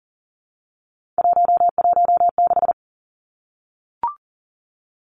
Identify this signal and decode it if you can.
Morse code, heard as a keyed tone, about 40 wpm, 700 Hz. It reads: 116
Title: Numbers - 3-Digit 40wpm